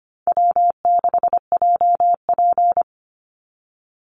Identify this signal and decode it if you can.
Morse code, heard as a keyed tone, about 25 wpm, 700 Hz. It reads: W6JP
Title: NA SOTA Chaser Callsigns 25wpm